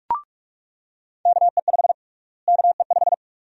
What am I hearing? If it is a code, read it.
Morse code, heard as a keyed tone, about 43 wpm, 700 Hz. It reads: KE5 KE5
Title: USA Call Sign Prefixes - Repeated 2x - 45wpm